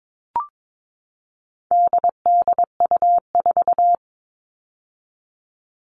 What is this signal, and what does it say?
Morse code, heard as a keyed tone, about 22 wpm, 700 Hz. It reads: DDU4
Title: All Cut Numbers - 4-Digit 22wpm